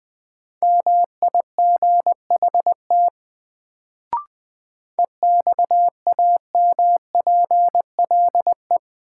Morse code, heard as a keyed tone, about 20 wpm, 700 Hz. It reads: MIGHT EXAMPLE
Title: Sets of 2 Words - New Word Drill - 20wpm